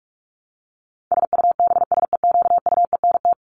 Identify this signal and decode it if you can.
Morse code, heard as a keyed tone, about 44 wpm, 700 Hz. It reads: SUBSEQUENT
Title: LY Suffix - Top 100 Words with Base Words 45wpm